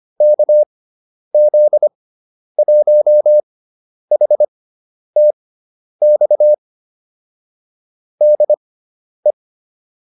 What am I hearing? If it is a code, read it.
Morse code, heard as a keyed tone, about 25 wpm, spaced as for 10 wpm, 600 Hz. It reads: KZ1HTX DE
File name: QSO301 10